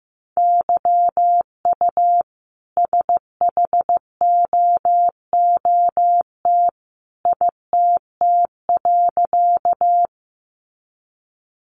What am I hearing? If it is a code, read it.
Morse code, heard as a keyed tone, about 15 wpm, 700 Hz. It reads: YU SHOOT ITT.
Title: A Sound Of Thunder 15wpm